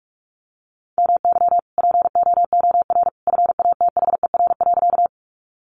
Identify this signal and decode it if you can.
Morse code, heard as a keyed tone, about 44 wpm, 700 Hz. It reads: MY POOR FATHER.
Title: Sentences from Top 1000 Words 45wpm